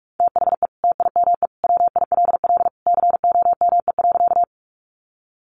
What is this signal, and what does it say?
Morse code, heard as a keyed tone, about 44 wpm, 700 Hz. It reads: THE TIME WILL COME.
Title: Sentences from Top 100 Words 45wpm